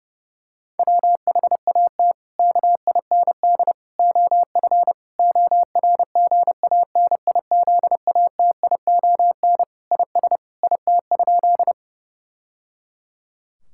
Morse code, heard as a keyed tone, about 30 wpm, 700 Hz. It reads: WHAT KIND OF ORGANIZATION IS IT?
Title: Sentences from Top 1000 Words - Review Part 5 - 30wpm